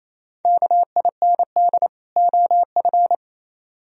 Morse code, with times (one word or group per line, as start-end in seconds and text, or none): 0.5-1.9 KIND
2.2-3.1 OF